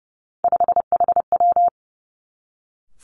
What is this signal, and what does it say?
Morse code, heard as a keyed tone, about 30 wpm, 700 Hz. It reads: 5HW
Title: Three Challenging Letter-Number Combinations 30wpm